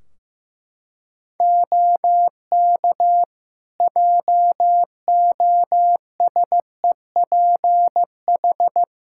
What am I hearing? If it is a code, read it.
Morse code, heard as a keyed tone, about 15 wpm, 700 Hz. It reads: OK JOSEPH
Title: Practice QSO - Exchange 3 - More Info - 15wpm